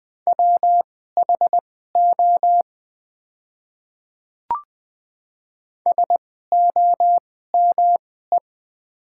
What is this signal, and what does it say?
Morse code, heard as a keyed tone, about 20 wpm, spaced as for 14 wpm, 700 Hz. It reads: WHO SOME
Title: 2x Character Spacing - Top 100 Words - 20wpm